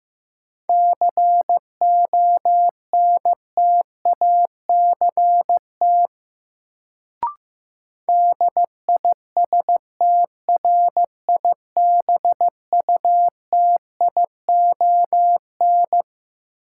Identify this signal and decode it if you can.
Morse code, heard as a keyed tone, about 15 wpm, 700 Hz. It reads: CONTACT DISTRIBUTION
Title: Sentences from Top 1000 Words - Review Part 7 - New Word Drill - 15wpm